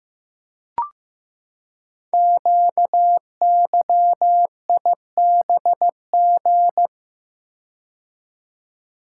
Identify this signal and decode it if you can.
Morse code, heard as a keyed tone, about 15 wpm, 700 Hz. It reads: QYIBG
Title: Code Groups - 5 Letters - 15wpm